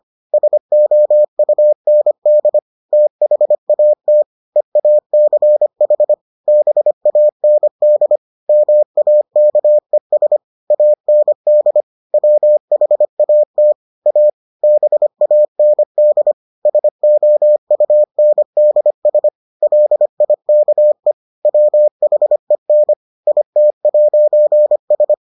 Morse code, heard as a keyed tone, about 25 wpm, 600 Hz. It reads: SOUND THAT EACH BAND MAKES AND WHAT A BAND SOUNDS LIKE WHEN IT'S